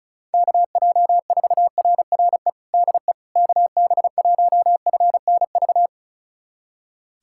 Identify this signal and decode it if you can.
Morse code, heard as a keyed tone, about 36 wpm, 700 Hz. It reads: KJ4RRE DE KB1FNV
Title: USA Callsign DE Callsign - 6 Characters - 35wpm